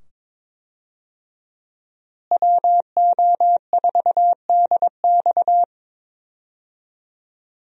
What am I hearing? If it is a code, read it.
Morse code, heard as a keyed tone, about 22 wpm, 700 Hz. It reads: WO4DX
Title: USA Call Signs - 5 Characters - Repeated 2x - 22wpm